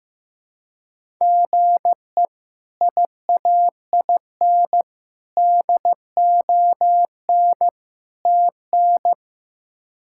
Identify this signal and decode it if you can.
Morse code, heard as a keyed tone, about 16 wpm, 700 Hz. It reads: GE IAIN DON TN
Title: K1USN SST QSO Practice - 15wpm